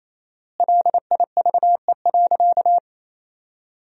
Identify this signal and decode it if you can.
Morse code, heard as a keyed tone, about 28 wpm, 700 Hz. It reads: LIVE.
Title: 4x Word-Spacing - Sentences from Top 300 Words - 28wpm